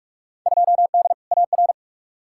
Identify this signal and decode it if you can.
Morse code, heard as a keyed tone, about 44 wpm, 700 Hz. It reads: 2D AR
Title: Field Day Exchanges Encore - 45wpm